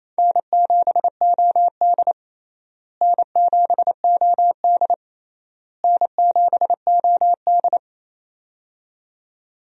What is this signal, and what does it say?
Morse code, heard as a keyed tone, about 28 wpm, 700 Hz. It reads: N7OD N7OD N7OD
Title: USA Call Signs - 4 Characters - Repeated 3x - 28wpm